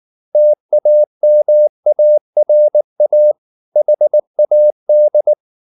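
Morse code, heard as a keyed tone, about 19 wpm, 600 Hz. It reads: TAMARA HAD
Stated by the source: Morse code at 19 wpm from CW med Gnister, reading Ken Follett's novel Never in English.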